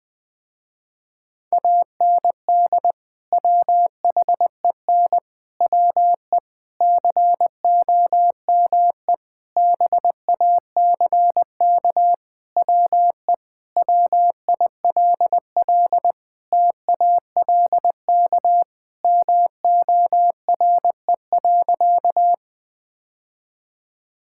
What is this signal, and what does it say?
Morse code, heard as a keyed tone, about 21 wpm, 700 Hz. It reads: AND WHEN WE COME BACK WE WILL TALK MORE.
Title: Sentences with 5-Word Phrases - Spoken Beginning - 20wpm